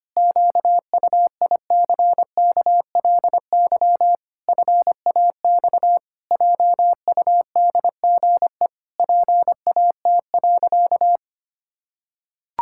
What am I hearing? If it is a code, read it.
Morse code, heard as a keyed tone, about 25 wpm, 700 Hz. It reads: QUICKLY FAX JUDGE PAT.
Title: Pangram Challenge v2 - 25wpm